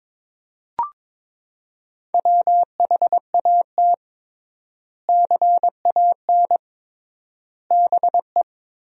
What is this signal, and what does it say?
Morse code, heard as a keyed tone, about 22 wpm, 700 Hz. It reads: WHAT CAN BE
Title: 3x Word-Spacing - Sentences from Top 500 Words - 22wpm